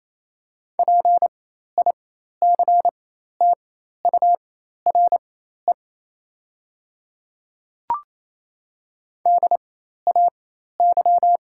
Morse code, heard as a keyed tone, about 28 wpm, spaced as for 13 wpm, 700 Hz. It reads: PICTURE DAY
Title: 4x Character Spacing - Top 200 Words - 28wpm